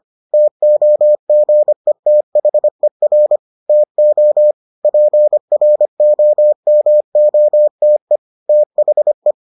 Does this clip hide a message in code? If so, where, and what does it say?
Yes: Morse code from 0.3 to 9.3 s, heard as a keyed tone, about 25 wpm, 600 Hz. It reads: TOGETHER TO PROMOTE THE